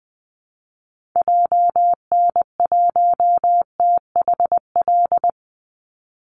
Morse code, heard as a keyed tone, about 19 wpm, 700 Hz. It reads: JN1THL
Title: Call Signs - 20wpm